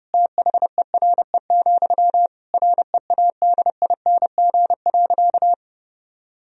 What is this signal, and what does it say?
Morse code, heard as a keyed tone, about 30 wpm, 700 Hz. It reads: THERE, READING.
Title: Sentences from Top 1000 Words - Review Part 3 - 30wpm